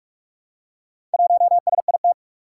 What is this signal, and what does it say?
Morse code, heard as a keyed tone, about 44 wpm, 700 Hz. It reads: 1SIT
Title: Four Challenging Letter-Number Combinations 45wpm